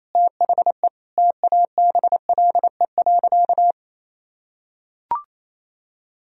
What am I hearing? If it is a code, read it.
Morse code, heard as a keyed tone, about 28 wpm, 700 Hz. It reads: THE TABLE.
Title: Sentences from Top 1000 Words - Review Part 3 - 28wpm